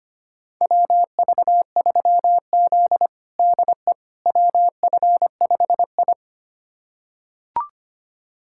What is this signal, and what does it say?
Morse code, heard as a keyed tone, about 25 wpm, 700 Hz. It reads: WV3Z DE WF5I
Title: USA Callsign DE Callsign - 4 Characters - 25wpm